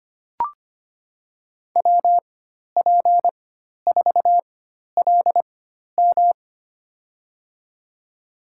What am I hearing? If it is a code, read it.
Morse code, heard as a keyed tone, about 25 wpm, spaced as for 12 wpm, 700 Hz. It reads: WP4LM
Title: USA Call Signs - 5 Characters - 4x Character Spacing - 25wpm